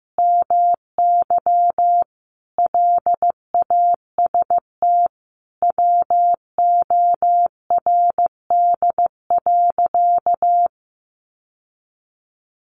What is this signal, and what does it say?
Morse code, heard as a keyed tone, about 15 wpm, 700 Hz. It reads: MY LAST WORD.
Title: Sentences from Top 1200 Words 15wpm